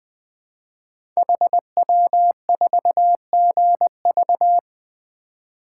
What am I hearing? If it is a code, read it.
Morse code, heard as a keyed tone, about 20 wpm, 700 Hz. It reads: HW4GV
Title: Code Groups - 5 Letter:Numbers - 20wpm